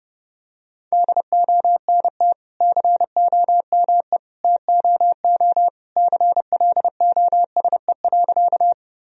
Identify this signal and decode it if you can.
Morse code, heard as a keyed tone, about 30 wpm, 700 Hz. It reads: DONT COME TOO CLOSE.
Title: Sentences from Top 1000 Words - Review Part 1 - 30wpm